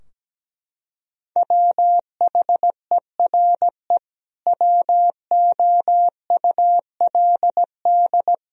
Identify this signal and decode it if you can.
Morse code, heard as a keyed tone, about 17 wpm, 700 Hz. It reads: WHERE WOULD
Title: Sentences from Top 1000 Words - Review Part 3 - 17wpm